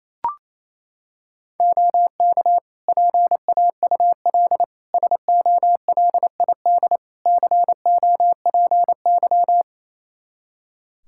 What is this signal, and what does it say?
Morse code, heard as a keyed tone, about 27 wpm, 700 Hz. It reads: OK PAUL SOLID COPY
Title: K1USN SST QSO Practice 28wpm